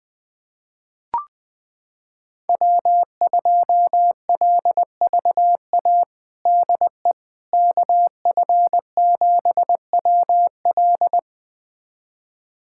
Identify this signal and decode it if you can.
Morse code, heard as a keyed tone, about 20 wpm, 700 Hz. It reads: W2LVA DE KF7WL
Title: USA Callsign DE Callsign - 5 Characters - 20wpm